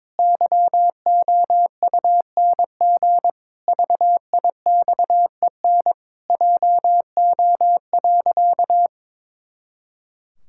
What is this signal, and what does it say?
Morse code, heard as a keyed tone, about 22 wpm, 700 Hz. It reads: YOUNG VIXEN JO.
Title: Pangram Challenge v2 - 22wpm